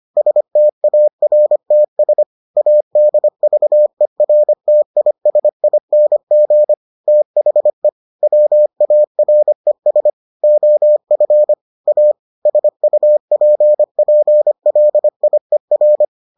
Morse code, heard as a keyed tone, about 25 wpm, 600 Hz. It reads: STARTS ADVERTISING THE WARES OF A SUPPLIER